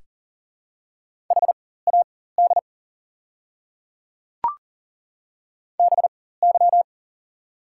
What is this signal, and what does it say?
Morse code, heard as a keyed tone, about 40 wpm, spaced as for 19 wpm, 690 Hz. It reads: HAD BY
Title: 4x Character Spacing - Top 100 Words - 40wpm